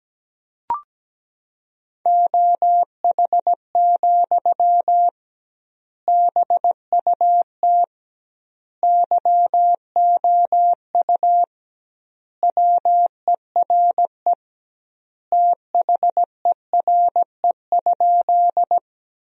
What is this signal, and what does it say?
Morse code, heard as a keyed tone, about 17 wpm, 700 Hz. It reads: OH, BUT YOU WERE THERE?
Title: 2x Word-Spacing - Sentences from Top 500 Words - 17wpm